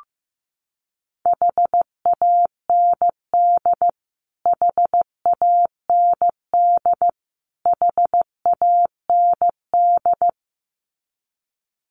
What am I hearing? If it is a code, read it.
Morse code, heard as a keyed tone, about 15 wpm, 700 Hz. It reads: HAND HAND HAND
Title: Top 500 Words Repeated 15wpm